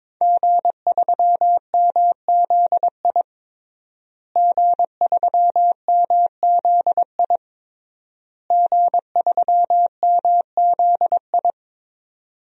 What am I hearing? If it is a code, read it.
Morse code, heard as a keyed tone, about 22 wpm, 700 Hz. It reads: G3MZI G3MZI G3MZI
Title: DXCC-100 Call Signs - 5 Characters - Repeated 3x - 22wpm